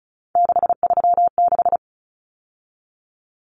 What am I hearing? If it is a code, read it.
Morse code, heard as a keyed tone, about 34 wpm, 700 Hz. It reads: 636
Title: Numbers - 3-Digit 35wpm